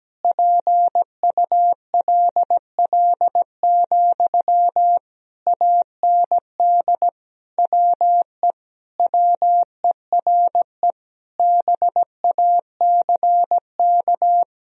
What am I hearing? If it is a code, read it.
Morse code, heard as a keyed tone, about 17 wpm, 700 Hz. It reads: PULL, AND WE WERE BACK
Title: Sentences from Top 1000 Words - Review Part 2 - 17wpm